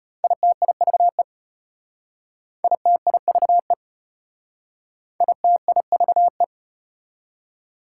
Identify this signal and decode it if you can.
Morse code, heard as a keyed tone, about 33 wpm, 700 Hz. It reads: ITIVE ITIVE ITIVE